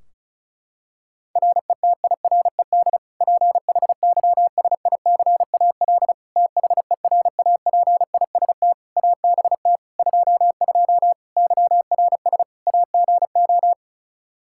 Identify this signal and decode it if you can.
Morse code, heard as a keyed tone, about 36 wpm, 700 Hz. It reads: RETIRED PHYSICAL THERAPIST ABT 22 YRS AGO
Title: Practice QSO - Exchange 3 - More Info - 35wpm